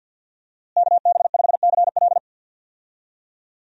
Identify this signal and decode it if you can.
Morse code, heard as a keyed tone, about 52 wpm, 700 Hz. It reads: KB5CL
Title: USA Call Signs - 5 Characters - Repeated 3x - 50wpm